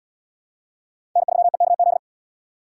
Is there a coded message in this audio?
Yes: Morse code from 1.2 to 2.0 s, heard as a keyed tone, about 69 wpm, 700 Hz. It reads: N3EKZ